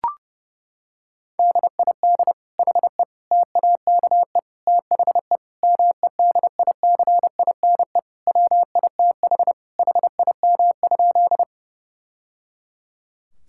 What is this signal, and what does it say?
Morse code, heard as a keyed tone, about 32 wpm, 700 Hz. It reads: DID HE TAKE THE MEDICINE WITH HIM?
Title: Sentences from Top 1000 Words - Review Part 6 - 30wpm